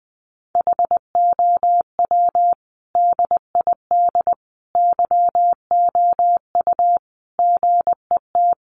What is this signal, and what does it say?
Morse code, heard as a keyed tone, about 20 wpm, 700 Hz. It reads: HOW DID YOU GET
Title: Sentences from Top 1100 Words 20wpm